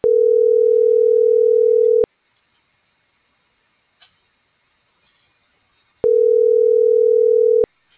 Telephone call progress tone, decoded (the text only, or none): ringback tone